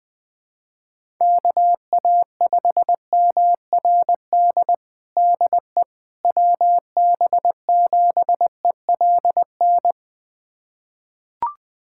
Morse code, heard as a keyed tone, about 20 wpm, 700 Hz. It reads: KA5MRD DE WB7ELN
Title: USA Callsign DE Callsign 20wpm